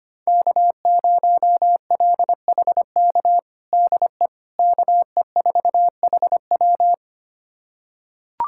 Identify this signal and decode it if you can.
Morse code, heard as a keyed tone, about 25 wpm, 700 Hz. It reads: K0LHK DE KE4HW
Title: USA Callsign DE Callsign - 5 Characters - 25wpm